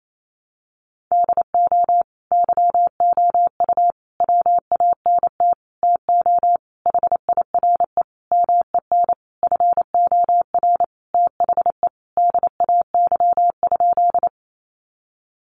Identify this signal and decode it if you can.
Morse code, heard as a keyed tone, about 28 wpm, 700 Hz. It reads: DO YOU WANT TO HIRE MEN FOR THE DAY?
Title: Sentences from Top 1400 Words 28wpm